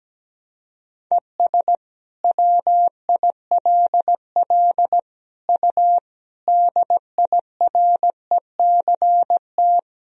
Morse code, heard as a keyed tone, about 17 wpm, 700 Hz. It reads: ES WILL U DIRECT
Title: Sentences from Top 400 Words and Abbreviations 17wpm